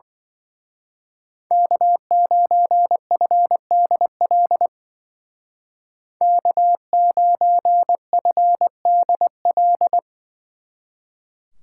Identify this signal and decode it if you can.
Morse code, heard as a keyed tone, about 24 wpm, 700 Hz. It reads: K9FDL K9FDL
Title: Speed Racing - USA 1x3 Call Signs 20wpm